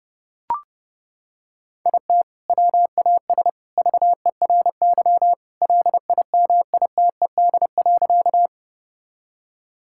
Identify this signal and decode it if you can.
Morse code, heard as a keyed tone, about 30 wpm, 700 Hz. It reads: IT WAS VERY LIMITED.
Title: Sentences from Top 1000 Words - Review Part 6 - 30wpm